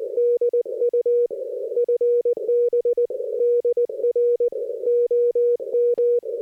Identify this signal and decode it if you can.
Morse code, heard as a keyed tone, about 20 wpm, 480 Hz. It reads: DU FB DR OM
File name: 60m qso